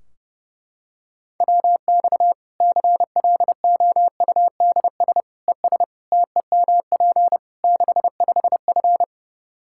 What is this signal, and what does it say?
Morse code, heard as a keyed tone, about 30 wpm, 700 Hz. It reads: WX CLOUDS ES TEMP 65F
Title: Practice QSO - Exchange 2 - Info - 30wpm